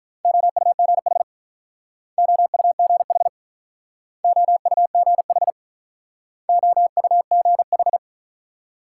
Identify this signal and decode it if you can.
Morse code, heard as a keyed tone, about 47 wpm, 700 Hz. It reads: OUGH OUGH OUGH OUGH